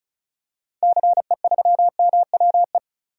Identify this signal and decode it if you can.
Morse code, heard as a keyed tone, about 34 wpm, 700 Hz. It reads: CE3MWE
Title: DXCC-100 Call Signs - 6 Characters - 35wpm